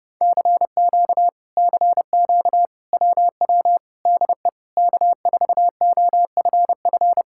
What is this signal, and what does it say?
Morse code, heard as a keyed tone, about 30 wpm, 700 Hz. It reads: CQ CQ WW DE K4OFF